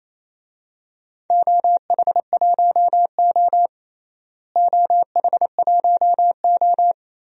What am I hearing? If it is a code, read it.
Morse code, heard as a keyed tone, about 28 wpm, 700 Hz. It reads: OH1O OH1O
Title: DXCC-100 Call Signs - 4 Characters - Repeated 3x - 28wpm